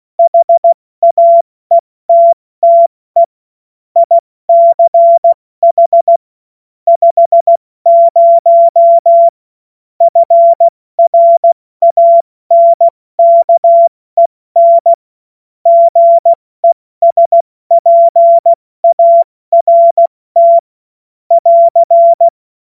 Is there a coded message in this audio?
Yes: Morse code from 0.2 to 22.3 s, heard as a keyed tone, about 16 wpm, 680 Hz. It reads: HAETTE ICH 50 FRANKEN GESPART <AR>